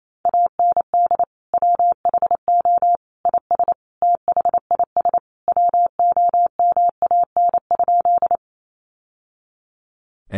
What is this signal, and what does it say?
Morse code, heard as a keyed tone, about 28 wpm, 700 Hz. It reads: AND WHO IS THIS WOMAN?
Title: Sentences from Top 1100 Words 28wpm